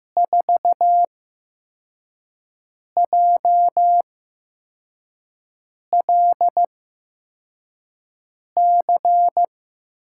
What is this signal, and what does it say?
Morse code, heard as a keyed tone, about 15 wpm, spaced as for 4 wpm, 700 Hz. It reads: 4JLC
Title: USA Call Signs - 5 Characters - 8x Character Spacing - 15wpm